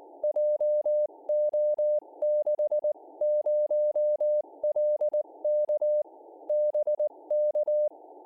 Morse code, heard as a keyed tone, about 19 wpm, 600 Hz. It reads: JO60LK BK